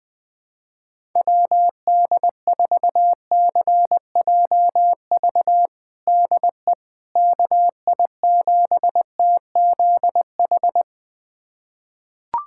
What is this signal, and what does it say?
Morse code, heard as a keyed tone, about 20 wpm, 700 Hz. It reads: WD4CJV DE KI7TZH
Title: USA Callsign DE Callsign - 6 Characters - 20wpm